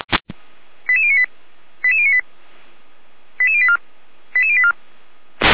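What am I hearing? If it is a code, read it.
Five-tone selective call (ZVEI), heard as quick runs of five tones, tones of about 80 ms. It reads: 88098 88098 88084 88084
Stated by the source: ZVEI five-tone call received on SDR.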